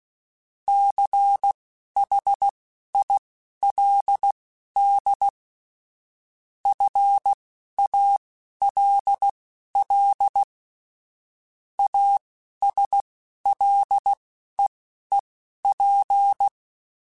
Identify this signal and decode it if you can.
Morse code, heard as a keyed tone, about 16 wpm, spaced as for 11 wpm, 800 Hz. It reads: CHILD FALL ASLEEP